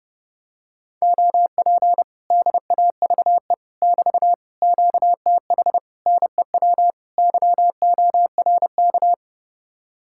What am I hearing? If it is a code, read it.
Morse code, heard as a keyed tone, about 29 wpm, 700 Hz. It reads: OP DAVE <BT> QTH NEW YORK